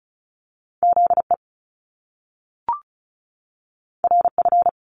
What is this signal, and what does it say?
Morse code, heard as a keyed tone, about 36 wpm, 700 Hz. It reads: ZE RF
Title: Top Two Letter Combinations in Words 35wpm